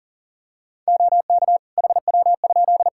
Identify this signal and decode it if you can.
Morse code, heard as a keyed tone, about 40 wpm, 700 Hz. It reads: OK HW?
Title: K1USN SST QSO Practice 40wpm